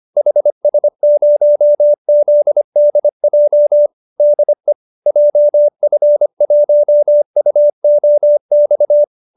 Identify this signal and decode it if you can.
Morse code, heard as a keyed tone, about 25 wpm, 590 Hz. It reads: HS0ZDJ DE JF1UOX